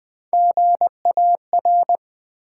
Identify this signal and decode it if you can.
Morse code, heard as a keyed tone, about 20 wpm, 700 Hz. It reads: GAR
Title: Build A Word - 3 Letters At A Time - 6-Letter Words - 20wpm